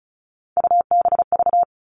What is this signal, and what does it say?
Morse code, heard as a keyed tone, about 34 wpm, 700 Hz. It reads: UBV
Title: All Cut Numbers - 3-Digit 35wpm